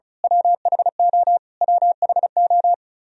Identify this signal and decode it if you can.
Morse code, heard as a keyed tone, about 36 wpm, 700 Hz. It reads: WHO WHO
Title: Top 100 Words Repeated 35wpm